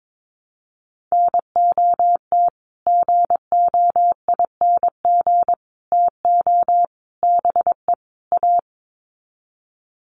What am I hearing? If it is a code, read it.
Morse code, heard as a keyed tone, about 22 wpm, 700 Hz. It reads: NOT GOING TO BE A
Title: Sets of 5 Words Encore 22wpm